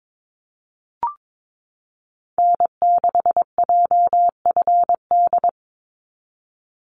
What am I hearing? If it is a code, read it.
Morse code, heard as a keyed tone, about 22 wpm, 700 Hz. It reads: N6JFD
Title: NA SOTA Activator Callsigns 22wpm